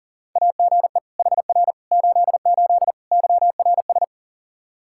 Morse code, heard as a keyed tone, about 37 wpm, 700 Hz. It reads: AGE HR 88 YRS
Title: Practice QSO - Exchange 3 - More Info - 40wpm